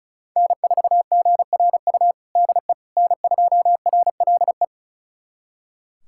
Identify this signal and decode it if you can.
Morse code, heard as a keyed tone, about 36 wpm, 690 Hz. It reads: N4GRU DE N2RLE
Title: USA Callsign DE Callsign - 5 Characters - 35wpm